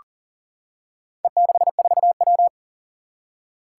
Morse code, heard as a keyed tone, about 42 wpm, 700 Hz. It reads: E64W
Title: Four Challenging Letter-Number Combinations 40wpm